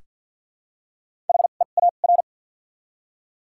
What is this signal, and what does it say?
Morse code, heard as a keyed tone, about 53 wpm, spaced as for 37 wpm, 700 Hz. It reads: HEAR
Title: 2x Character Spacing - Top 300 Words - 50wpm